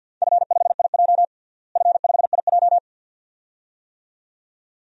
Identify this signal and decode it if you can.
Morse code, heard as a keyed tone, about 49 wpm, 700 Hz. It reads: F5IJ F5IJ
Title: DXCC-100 Call Signs - 4 Characters - Repeated 2x - 50wpm